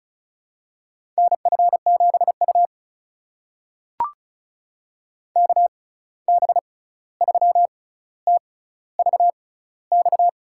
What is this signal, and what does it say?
Morse code, heard as a keyed tone, about 36 wpm, spaced as for 16 wpm, 700 Hz. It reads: NF7U KB3TVX
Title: USA Call Signs - 6x Character Spacing - 35wpm